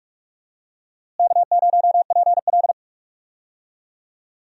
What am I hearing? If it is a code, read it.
Morse code, heard as a keyed tone, about 43 wpm, 700 Hz. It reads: K0PL
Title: USA Call Signs - 4 Characters - Repeated 3x - 45wpm